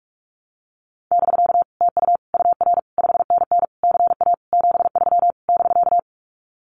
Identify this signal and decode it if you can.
Morse code, heard as a keyed tone, about 44 wpm, 700 Hz. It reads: <BK> TU UR 5NN CA 73 <BK>
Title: Common QSO Phrases Encore 45wpm